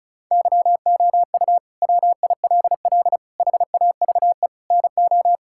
Morse code, heard as a keyed tone, about 36 wpm, 700 Hz. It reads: YOU WILL HAVE NO
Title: Sentences from Top 1000 Words - Review Part 6 - 35wpm